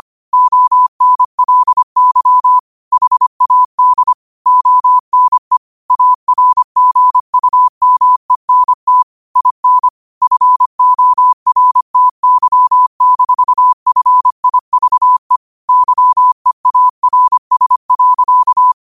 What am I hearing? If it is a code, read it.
Morse code, heard as a keyed tone, about 25 wpm, 1000 Hz. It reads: ONLY HAD ONE ARGUMENT IN FORTY-FIVE YEARS.
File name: qotdfiles / 25wpm